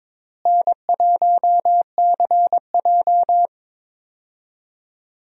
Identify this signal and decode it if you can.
Morse code, heard as a keyed tone, about 22 wpm, 700 Hz. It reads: N1CJ
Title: USA Call Signs - 4 Characters - Repeated 2x - 22wpm